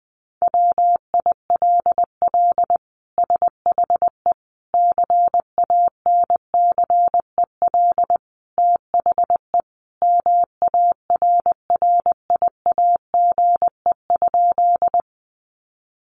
Morse code, heard as a keyed tone, about 20 wpm, 700 Hz. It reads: WILL SHE CANCEL THE MARRIAGE?
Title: Sentences from Top 1200 Words 20wpm